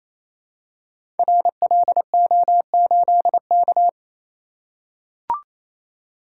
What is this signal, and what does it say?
Morse code, heard as a keyed tone, about 28 wpm, 700 Hz. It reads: RLO8K
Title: Code Groups - 5 Letter:Numbers - 28wpm